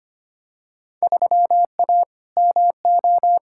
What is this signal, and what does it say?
Morse code, heard as a keyed tone, about 25 wpm, 700 Hz. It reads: 3A MO
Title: Field Day Exchanges - 25wpm